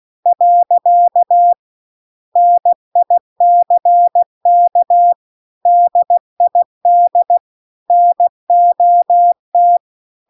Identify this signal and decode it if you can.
Morse code, heard as a keyed tone, about 16 wpm, 700 Hz. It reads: . NICK DID NOT